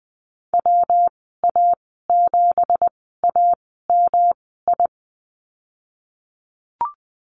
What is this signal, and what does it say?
Morse code, heard as a keyed tone, about 20 wpm, spaced as for 15 wpm, 700 Hz. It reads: WA7AMI
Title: 179 Lesson 29g - Slash - Call Signs 20wpm